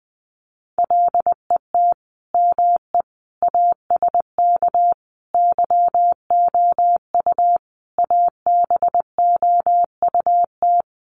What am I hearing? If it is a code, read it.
Morse code, heard as a keyed tone, about 20 wpm, 700 Hz. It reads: LET ME ASK YOU ABOUT
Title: Sets of 5 Words 20wpm